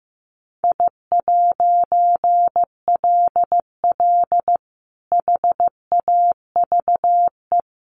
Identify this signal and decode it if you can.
Morse code, heard as a keyed tone, about 15 wpm, 700 Hz. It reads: I'LL HAVE
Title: A Sound Of Thunder 15wpm